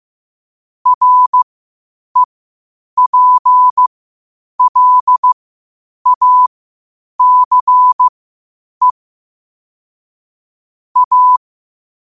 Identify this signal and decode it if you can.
Morse code, heard as a keyed tone, about 15 wpm, spaced as for 8 wpm, 1000 Hz. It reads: REPLACE A
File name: qotdfiles / 8wpm